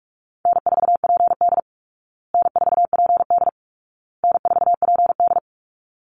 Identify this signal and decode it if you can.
Morse code, heard as a keyed tone, about 44 wpm, 700 Hz. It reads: N4PD N4PD N4PD
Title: Easy Call Signs Repeated 3x - 45wpm